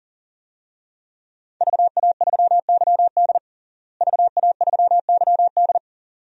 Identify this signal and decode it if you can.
Morse code, heard as a keyed tone, about 37 wpm, 700 Hz. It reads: VA3YD VA3YD
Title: DXCC-100 Call Signs - 5 Characters - Repeated 2x - 40wpm